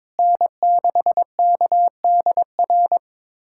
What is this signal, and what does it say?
Morse code, heard as a keyed tone, about 22 wpm, 700 Hz. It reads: N6KDR
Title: USA Callsign DE Callsign - 5 Characters - 22wpm